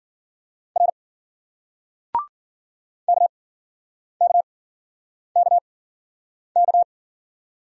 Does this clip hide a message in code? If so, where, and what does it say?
Yes: Morse code from 0.8 to 6.8 s, heard as a keyed tone, about 54 wpm, spaced as for 9 wpm, 700 Hz. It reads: R KKKK